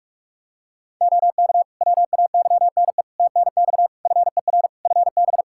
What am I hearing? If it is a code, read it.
Morse code, heard as a keyed tone, about 42 wpm, 700 Hz. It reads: OK WAYNE TNX FER FB
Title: Practice QSO - Exchange 4 - Ending - 45wpm